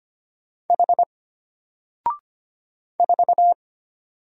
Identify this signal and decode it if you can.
Morse code, heard as a keyed tone, about 25 wpm, 700 Hz. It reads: H 4
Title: Single Letter-Number Alt 25wpm